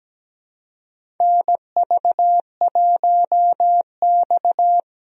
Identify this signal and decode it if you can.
Morse code, heard as a keyed tone, about 17 wpm, 700 Hz. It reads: NV1X
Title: USA Call Signs - 4 Characters - 17wpm